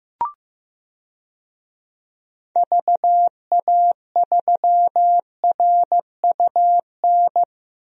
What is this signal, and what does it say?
Morse code, heard as a keyed tone, about 15 wpm, 700 Hz. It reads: VA3RUN
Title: DXCC-100 Call Signs - 6 Characters - Repeated 3x - 15wpm